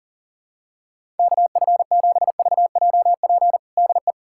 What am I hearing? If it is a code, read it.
Morse code, heard as a keyed tone, about 40 wpm, 700 Hz. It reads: KF7VJP DE
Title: USA Callsign DE Callsign 40wpm